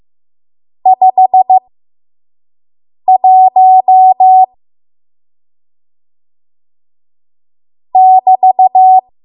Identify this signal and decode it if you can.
Morse code, heard as a keyed tone, about 15 wpm, spaced as for 5 wpm, 750 Hz. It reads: 51 <BT>
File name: Morse / 05WPM